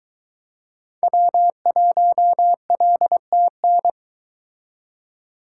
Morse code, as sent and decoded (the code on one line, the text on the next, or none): .-- .---- .-.. - -.
W1LTN